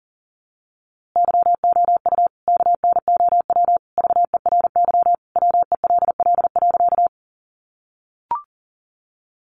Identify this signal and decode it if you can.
Morse code, heard as a keyed tone, about 40 wpm, 700 Hz. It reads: YOU KNOW VERY WELL.
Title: A Sound Of Thunder 40wpm